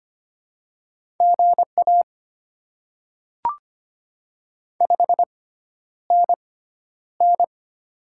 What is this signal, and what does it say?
Morse code, heard as a keyed tone, about 25 wpm, spaced as for 11 wpm, 700 Hz. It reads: GA 5NN
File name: Top 20 QSO Elements - 6x Character Spacing - 25wpm